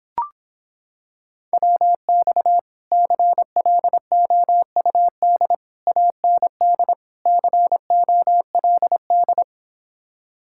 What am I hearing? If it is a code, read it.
Morse code, heard as a keyed tone, about 26 wpm, 700 Hz. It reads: WX CLOUD AND COLD